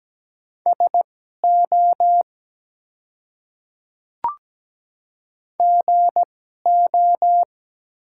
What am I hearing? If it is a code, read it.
Morse code, heard as a keyed tone, about 17 wpm, spaced as for 12 wpm, 700 Hz. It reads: SO GO
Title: 2x Character Spacing - Top 100 Words - 17wpm